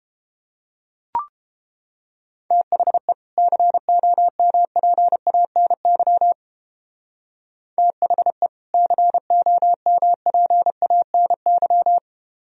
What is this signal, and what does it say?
Morse code, heard as a keyed tone, about 29 wpm, 700 Hz. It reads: THE COMPANY THE COMPANY